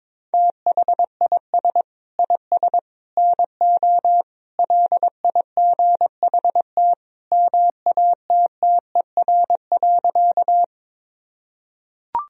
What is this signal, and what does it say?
Morse code, heard as a keyed tone, about 22 wpm, 700 Hz. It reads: THIS IS NO LIGHT MATTER.
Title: Sentences from Top 1000 Words - Review Part 1 - 22wpm